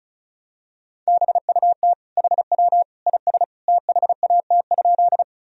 Morse code, heard as a keyed tone, about 34 wpm, 700 Hz. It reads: BUT HW IS THAT?
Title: Sentences from Top 100 Words and Abbreviations 35wpm